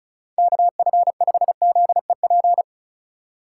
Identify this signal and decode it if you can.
Morse code, heard as a keyed tone, about 34 wpm, 700 Hz. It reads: KF5ZEP
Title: USA Call Signs - 6 Characters - 4x Character Spacing - 35wpm